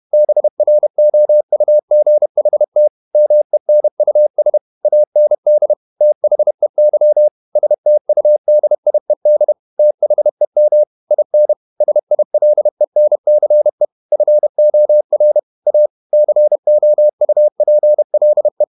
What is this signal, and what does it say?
Morse code, heard as a keyed tone, about 31 wpm, 600 Hz. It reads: BROUGHT MENUS AND THEY STUDIED THEM IN SILENCE FOR A COUPLE
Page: Never 31wpm | CW med Gnister